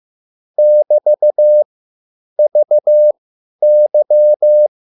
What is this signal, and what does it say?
Morse code, heard as a keyed tone, about 15 wpm, spaced as for 9 wpm, 600 Hz. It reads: <BT>VY
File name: QSL / CW lesson